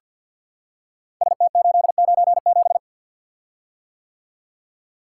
Morse code, heard as a keyed tone, about 45 wpm, 700 Hz. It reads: ST897
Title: Code Groups - 5 Letter:Numbers - 50wpm